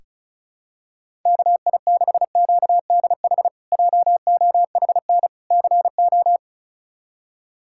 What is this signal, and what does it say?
Morse code, heard as a keyed tone, about 36 wpm, 700 Hz. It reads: KI6QDH JOHN CO
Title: K1USN SST QSO Practice - 35wpm